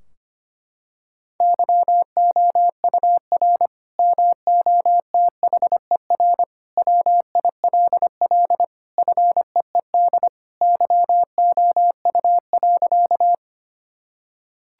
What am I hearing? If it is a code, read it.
Morse code, heard as a keyed tone, about 25 wpm, 700 Hz. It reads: YOUR MOTHER WILL FEED YOU.
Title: Sentences from Top 1600 Words 25wpm